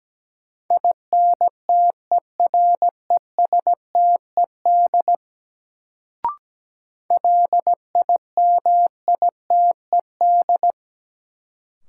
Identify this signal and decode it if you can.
Morse code, heard as a keyed tone, about 17 wpm, 700 Hz. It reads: INTERESTED LIMITED
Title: Sentences from Top 900 Words - New Word Drill - 17wpm